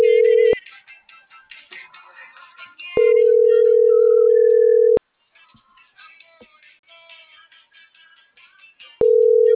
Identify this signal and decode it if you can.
Ringback tone (North America), 440 + 480 Hz.